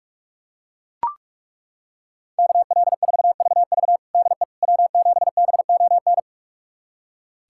Morse code, heard as a keyed tone, about 41 wpm, 700 Hz. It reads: KL4VV DE W7BON
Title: USA Callsign DE Callsign - 5 Characters - 45wpm